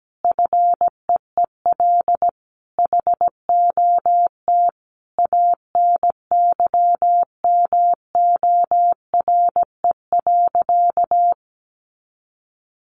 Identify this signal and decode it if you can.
Morse code, heard as a keyed tone, about 17 wpm, 700 Hz. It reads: FEEL HOT ANYMORE.
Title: Additional Easy Phrases 17wpm